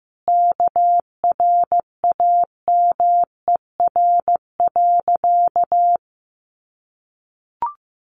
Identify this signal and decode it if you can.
Morse code, heard as a keyed tone, about 15 wpm, 700 Hz. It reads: KRAMER.
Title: A Sound Of Thunder 15wpm